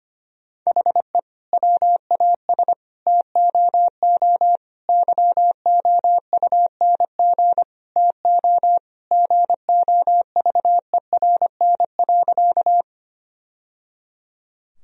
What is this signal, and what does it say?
Morse code, heard as a keyed tone, about 25 wpm, 700 Hz. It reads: HE WAS TOO YOUNG TO GOVERN.
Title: Sentences from Top 1000 Words - Review Part 2 - 25wpm